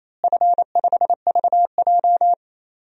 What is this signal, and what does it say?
Morse code, heard as a keyed tone, about 28 wpm, 700 Hz. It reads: F5VJ
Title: DXCC-100 Call Signs - 4 Characters - Repeated 2x - 28wpm